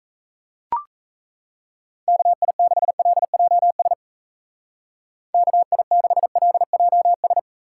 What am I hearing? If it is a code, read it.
Morse code, heard as a keyed tone, about 37 wpm, 700 Hz. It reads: KI6LJS KI6LJS